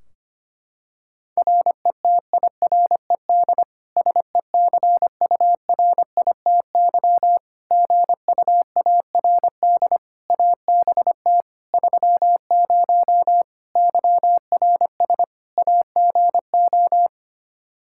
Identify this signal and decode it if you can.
Morse code, heard as a keyed tone, about 25 wpm, 700 Hz. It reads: RETIRED SECURITY GUARD ABT 30 YRS AGO
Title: K1USN SST QSO Practice 25wpm